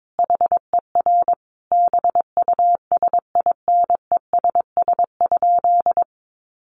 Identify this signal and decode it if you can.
Morse code, heard as a keyed tone, about 22 wpm, 700 Hz. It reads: HER BUSINESS?
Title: Sentences from Top 1100 Words 22wpm